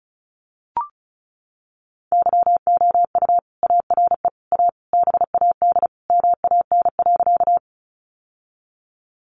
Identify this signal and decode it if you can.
Morse code, heard as a keyed tone, about 34 wpm, 700 Hz. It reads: YOU ARE A BAD MAN.
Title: Sentences from Top 700 Words 35wpm